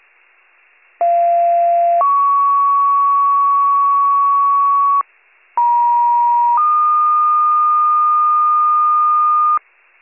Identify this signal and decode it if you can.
Two-tone page, heard as a one-second tone, then a three-second tone, about 690 then 1120 Hz, about 950 then 1220 Hz. It is heard from 1.0 to 9.6 s.